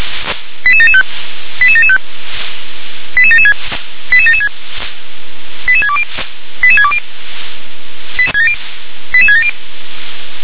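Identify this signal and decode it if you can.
Five-tone selective call (ZVEI), heard as quick runs of five tones, tones of about 70 ms. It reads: 88775 88775 88776 88776 88620 88620 88670
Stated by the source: ZVEI five-tone selective call received on an SDR.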